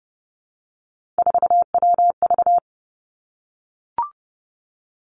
Morse code, heard as a keyed tone, about 30 wpm, 700 Hz. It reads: 4WV
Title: Three Challenging Letter-Number Combinations 30wpm